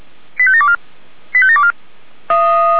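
Five-tone selective call (ZVEI), heard as quick runs of five tones, tones of about 70 ms. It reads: 85724 85724
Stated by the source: ZVEI five-tone selective call received via SDR.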